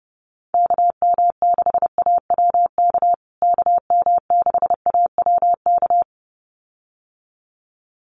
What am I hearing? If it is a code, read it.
Morse code, heard as a keyed tone, about 30 wpm, 700 Hz. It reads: KM6AWK KM6AWK
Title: USA Call Signs Repeated 2x 30wpm